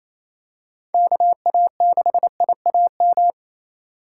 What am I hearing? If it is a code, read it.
Morse code, heard as a keyed tone, about 27 wpm, 700 Hz. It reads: KA6IAM
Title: USA Call Signs - 6 Characters - 4x Character Spacing - 28wpm